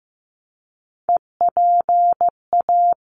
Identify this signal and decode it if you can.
Morse code, heard as a keyed tone, about 15 wpm, 700 Hz. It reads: EPA
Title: ARRL Sections 15wpm